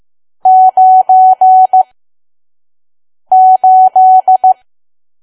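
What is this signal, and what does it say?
Morse code, heard as a keyed tone, about 15 wpm, spaced as for 5 wpm, 750 Hz. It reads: 98
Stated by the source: Morse code practice at 5 WPM from KQ2P.COM.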